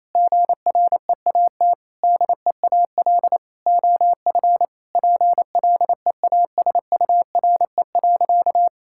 Morse code, heard as a keyed tone, about 28 wpm, 700 Hz. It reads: GREAT DEAL OF PLEASURE.
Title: Sentences from Top 1600 Words 28wpm